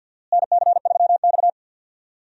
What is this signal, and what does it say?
Morse code, heard as a keyed tone, about 46 wpm, 700 Hz. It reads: NC3X
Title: USA Call Signs - 4 Characters - Repeated 2x - 50wpm